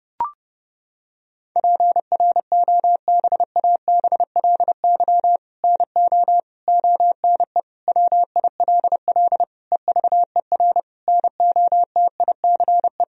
Morse code, heard as a keyed tone, about 30 wpm, 700 Hz. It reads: PROBABLY NO ONE WILL EVER NOTICE
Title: Sentences from Top 1000 Words - Review Part 4 - 30wpm